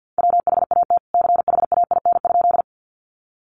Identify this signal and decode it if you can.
Morse code, heard as a keyed tone, about 50 wpm, 700 Hz. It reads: WHAT CHAIN?
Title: Sentences from Top 1500 Words 50wpm